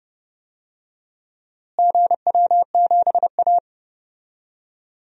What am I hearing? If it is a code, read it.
Morse code, heard as a keyed tone, about 31 wpm, 700 Hz. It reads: GW7A
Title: DXCC-100 Call Signs - 4 Characters - Repeated 2x - 30wpm